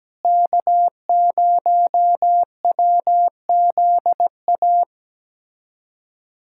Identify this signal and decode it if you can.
Morse code, heard as a keyed tone, about 17 wpm, 700 Hz. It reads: K0WZA
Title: USA Callsign DE Callsign 17wpm